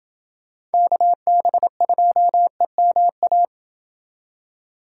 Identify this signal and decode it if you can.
Morse code, heard as a keyed tone, about 29 wpm, 700 Hz. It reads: KB2EMA